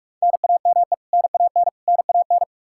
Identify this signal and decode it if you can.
Morse code, heard as a keyed tone, about 46 wpm, 700 Hz. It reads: NAME NAN NAN
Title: Practice QSO - Exchange 1 - Beginning - 45wpm